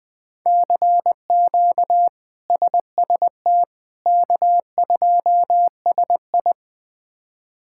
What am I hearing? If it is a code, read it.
Morse code, heard as a keyed tone, about 21 wpm, 700 Hz. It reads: CQ SST K2SI
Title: K1USN SST QSO Practice - 20wpm